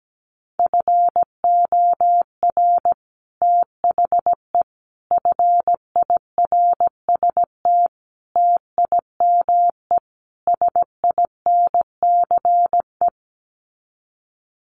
Morse code, heard as a keyed tone, about 17 wpm, 700 Hz. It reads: FOR THE FIRST TIME SINCE
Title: Sets of 5 Words 17wpm